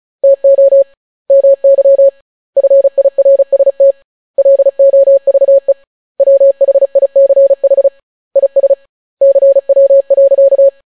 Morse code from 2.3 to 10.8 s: FIRST LOVE WHICH IS CW.